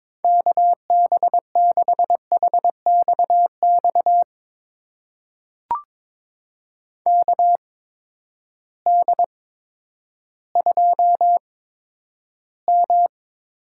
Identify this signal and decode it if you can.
Morse code, heard as a keyed tone, about 22 wpm, spaced as for 11 wpm, 700 Hz. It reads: KB6HXX KD2M
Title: USA Call Signs - 6 Characters - 8x Character Spacing - 22wpm